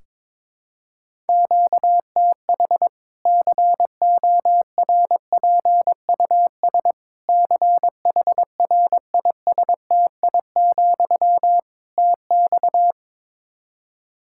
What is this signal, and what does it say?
Morse code, heard as a keyed tone, about 22 wpm, 700 Hz. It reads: QTH CORPUS CHRISTI, TX
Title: Practice QSO - Exchange 1 - Beginning - 22wpm